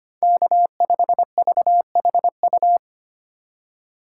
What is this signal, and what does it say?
Morse code, heard as a keyed tone, about 25 wpm, 700 Hz. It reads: K5VHU
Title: USA Call Signs - 5 Characters - 2x Character Spacing - 25wpm